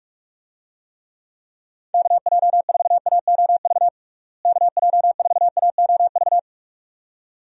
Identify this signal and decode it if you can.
Morse code, heard as a keyed tone, about 46 wpm, 700 Hz. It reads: KJ4AOV KJ4AOV
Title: USA Call Signs - 6 Characters - Repeated 2x - 45wpm